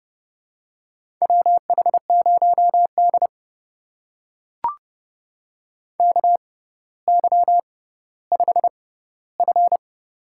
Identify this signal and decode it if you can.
Morse code, heard as a keyed tone, about 31 wpm, spaced as for 15 wpm, 700 Hz. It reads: WH0D KY5F
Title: USA Call Signs - 4 Characters - 6x Character Spacing - 30wpm